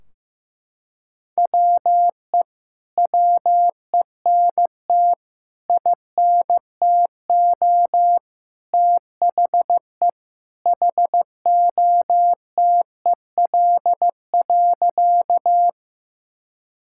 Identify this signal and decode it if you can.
Morse code, heard as a keyed tone, about 15 wpm, 700 Hz. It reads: WE WENT INTO THE HOTEL.
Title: Sentences from Top 1600 Words 15wpm